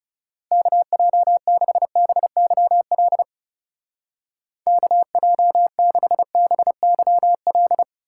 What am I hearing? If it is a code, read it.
Morse code, heard as a keyed tone, about 34 wpm, 700 Hz. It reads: KJ6BYL KJ6BYL
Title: Speed Racing - USA Call Signs 30wpm